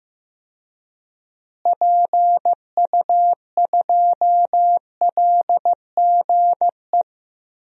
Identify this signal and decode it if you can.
Morse code, heard as a keyed tone, about 16 wpm, 700 Hz. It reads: PU2LGE
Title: DXCC-100 Call Signs - 6 Characters - Repeated 2x - 15wpm